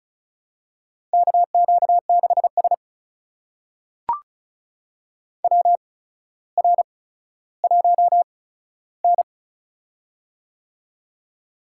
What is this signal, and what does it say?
Morse code, heard as a keyed tone, about 36 wpm, spaced as for 15 wpm, 700 Hz. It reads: KQ6S WR1N
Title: USA Call Signs - 4 Characters - 8x Character Spacing - 35wpm